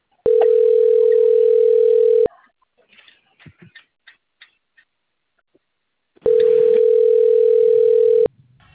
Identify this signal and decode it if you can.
Ringback tone (North America), 440 + 480 Hz.